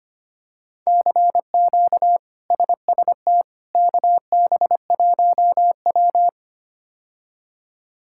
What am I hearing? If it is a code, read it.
Morse code, heard as a keyed tone, about 25 wpm, 700 Hz. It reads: CQ SST KB1W
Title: K1USN SST QSO Practice - 25wpm